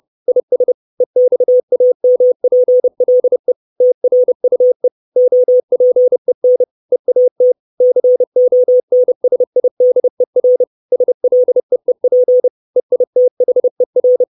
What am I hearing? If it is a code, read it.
Morse code, heard as a keyed tone, about 30 wpm, 500 Hz. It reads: IS EXAMPLE TRUE OPEN EAT CONSIDER SLEEP EITHER
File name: 632 Words 30WPM